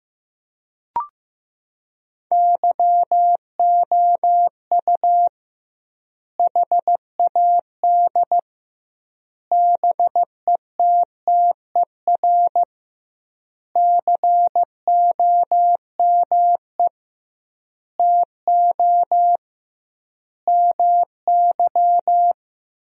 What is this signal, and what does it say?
Morse code, heard as a keyed tone, about 16 wpm, 700 Hz. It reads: YOU HAD BETTER COME TO MY
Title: 2x Word-Spacing - Sentences from Top 400 Words - 15wpm